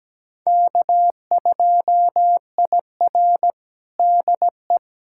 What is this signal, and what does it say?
Morse code, heard as a keyed tone, about 17 wpm, 700 Hz. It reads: K2IR DE
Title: USA Callsign DE Callsign - 4 Characters - 17wpm